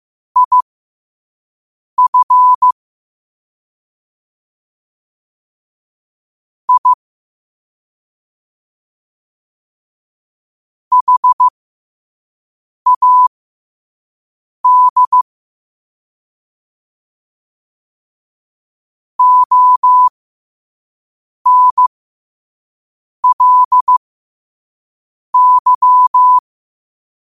Morse code, heard as a keyed tone, about 15 wpm, spaced as for 5 wpm, 1000 Hz. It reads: IF I HAD ONLY
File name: 5wpm